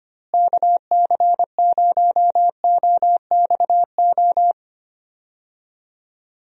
Morse code, heard as a keyed tone, about 25 wpm, 700 Hz. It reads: KC0OXO
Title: USA Call Signs - 6 Characters - Repeated 2x - 25wpm